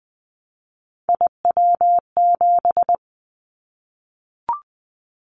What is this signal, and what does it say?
Morse code, heard as a keyed tone, about 20 wpm, 700 Hz. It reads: IW7
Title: Callsign Prefixes for Top 25 DXCC - Repeated 3x - 20wpm